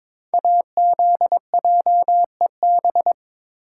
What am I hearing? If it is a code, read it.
Morse code, heard as a keyed tone, about 22 wpm, 700 Hz. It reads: AZJEB
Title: Code Groups - 5 Letter:Numbers - 22wpm